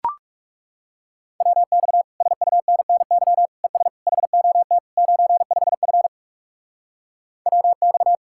Morse code, heard as a keyed tone, about 40 wpm, 700 Hz. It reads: WX SUNNY ES HOT 95F WX